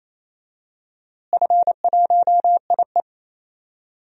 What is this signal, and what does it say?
Morse code, heard as a keyed tone, about 28 wpm, 700 Hz. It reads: F1IE
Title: DXCC-100 Call Signs - 4 Characters - Repeated 3x - 28wpm